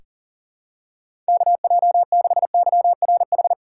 Morse code, heard as a keyed tone, about 39 wpm, 700 Hz. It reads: KJ6YRH
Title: USA Call Signs - 6 Characters - 4x Character Spacing - 40wpm